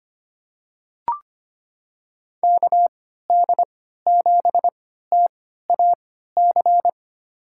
Morse code, heard as a keyed tone, about 25 wpm, spaced as for 14 wpm, 700 Hz. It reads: KD7TAC
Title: USA Call Signs - 6 Characters - 3x Character Spacing - 25wpm